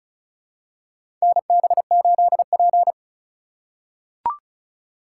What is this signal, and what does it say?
Morse code, heard as a keyed tone, about 36 wpm, 700 Hz. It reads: NB8P
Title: USA Call Signs - 4 Characters - 3x Character Spacing - 35wpm